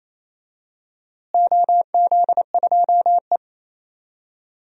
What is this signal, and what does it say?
Morse code, heard as a keyed tone, about 28 wpm, 700 Hz. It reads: OZ2E
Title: DXCC-100 Call Signs - 4 Characters - Repeated 2x - 28wpm